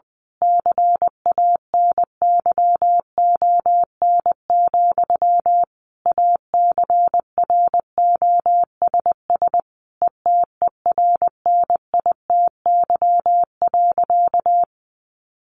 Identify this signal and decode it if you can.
Morse code, heard as a keyed tone, about 20 wpm, 700 Hz. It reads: CANYON, ACROSS ETERNITY.
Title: A Sound Of Thunder 20wpm